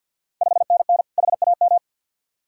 Morse code, heard as a keyed tone, about 51 wpm, 690 Hz. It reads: 5NN HAM
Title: Ontario QSO Party Exchange 50wpm